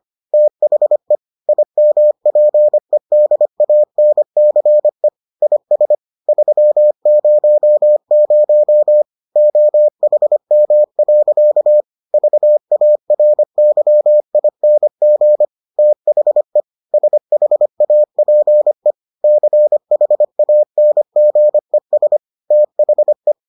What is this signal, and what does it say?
Morse code, heard as a keyed tone, about 25 wpm, 600 Hz. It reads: THE IMPEDANCE IS 300 OHM. VARYING THE SHAPE CHANGES THE